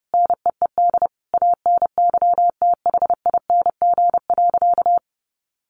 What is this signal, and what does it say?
Morse code, heard as a keyed tone, about 30 wpm, 700 Hz. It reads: NEED ANYTHING.
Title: Sentences from Top 1000 Words 30wpm